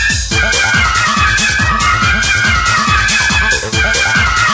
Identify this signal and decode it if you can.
Emergency Alert System (SAME) tones, heard as harsh buzzing bursts. It is heard from 0.3 to 4.3 s.